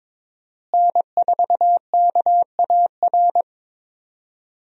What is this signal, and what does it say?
Morse code, heard as a keyed tone, about 22 wpm, 700 Hz. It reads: N4KAR
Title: USA Call Signs - 5 Characters - Repeated 2x - 22wpm